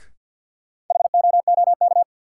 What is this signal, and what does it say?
Morse code, heard as a keyed tone, about 52 wpm, 700 Hz. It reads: HOOK
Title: Sentences from Top 1600 Words - New Word Drill Alt - 50wpm